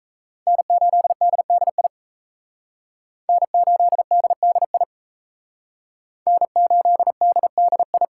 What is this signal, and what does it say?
Morse code, heard as a keyed tone, about 34 wpm, 700 Hz. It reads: N8DDI N8DDI N8DDI
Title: Speed Racing - USA Call Signs 28wpm